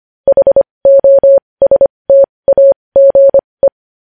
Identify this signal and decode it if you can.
Morse code, heard as a keyed tone, about 25 wpm, spaced as for 20 wpm, 550 Hz. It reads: HOSTAGE
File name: podcast / News headlines-1x-25x20